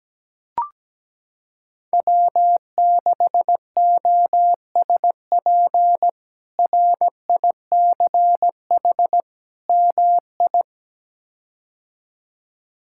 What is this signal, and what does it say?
Morse code, heard as a keyed tone, about 17 wpm, 700 Hz. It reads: W6OSP RICH MI
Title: K1USN SST QSO Practice - 17wpm